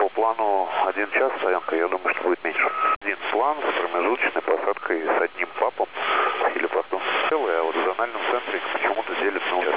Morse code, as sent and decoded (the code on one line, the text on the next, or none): none